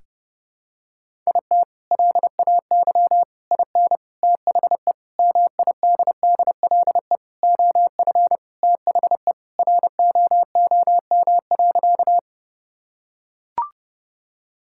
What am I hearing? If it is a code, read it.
Morse code, heard as a keyed tone, about 30 wpm, 700 Hz. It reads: IT LAY IN THE MIDDLE OF THE ROOM.
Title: Sentences from Top 1000 Words - Review Part 6 - 30wpm